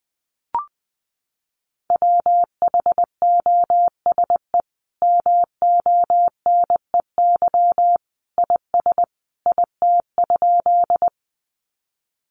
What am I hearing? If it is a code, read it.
Morse code, heard as a keyed tone, about 20 wpm, 700 Hz. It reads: WHOSE MONEY IS IT?
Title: Sentences from Top 1200 Words 20wpm